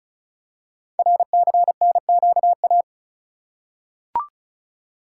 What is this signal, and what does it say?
Morse code, heard as a keyed tone, about 36 wpm, 700 Hz. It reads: RCNQA
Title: Code Groups - 5 Letters - 35wpm